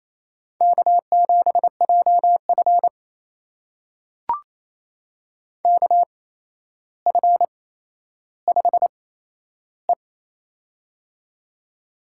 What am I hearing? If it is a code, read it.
Morse code, heard as a keyed tone, about 27 wpm, spaced as for 12 wpm, 700 Hz. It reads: K7JF KF5E
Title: USA Call Signs - 4 Characters - 8x Character Spacing - 28wpm